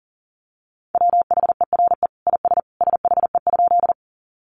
Morse code, heard as a keyed tone, about 40 wpm, 700 Hz. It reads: WHERE IS SHE?
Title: Sentences from Top 200 Words 40wpm